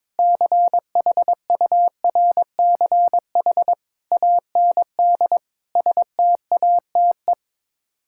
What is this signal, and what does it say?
Morse code, heard as a keyed tone, about 22 wpm, 700 Hz. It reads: CHURCH AND STATE
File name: practice / 22wpm / Build A Phrase - 5-Word Phrases - 22wpm